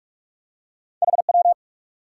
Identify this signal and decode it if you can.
Morse code, heard as a keyed tone, about 46 wpm, 700 Hz. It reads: HW
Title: Top 90 QSO Elements 45wpm